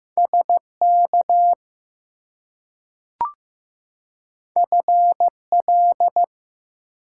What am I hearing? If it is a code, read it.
Morse code, heard as a keyed tone, about 15 wpm, 700 Hz. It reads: SK FL
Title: US States and Canadian Provinces 15wpm